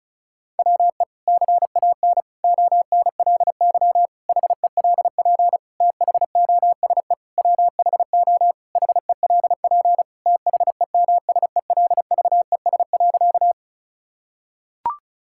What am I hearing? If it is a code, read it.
Morse code, heard as a keyed tone, about 36 wpm, 700 Hz. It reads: WE CAN ONLY HELP THOSE WHO HELP THEMSELVES.
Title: Sentences from Top 1000 Words - Review Part 5 - 35wpm